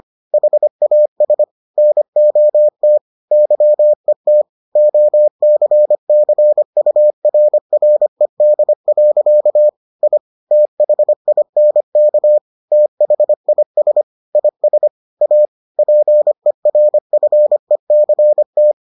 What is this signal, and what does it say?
Morse code, heard as a keyed tone, about 25 wpm, 600 Hz. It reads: HAS NOT YET OCCURRED. I THINK THIS IS A PERFECT